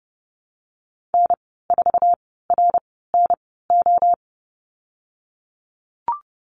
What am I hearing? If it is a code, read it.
Morse code, heard as a keyed tone, about 30 wpm, spaced as for 17 wpm, 700 Hz. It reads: N4RNO
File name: Lesson 14g - Number 2 - Call Signs 30wpm